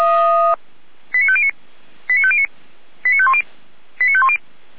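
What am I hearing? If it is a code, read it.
Five-tone selective call (ZVEI), heard as quick runs of five tones, tones of about 70 ms. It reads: 89509 89509 89510 89510
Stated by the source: ZVEI five-tone selective call 89150 received on SDR.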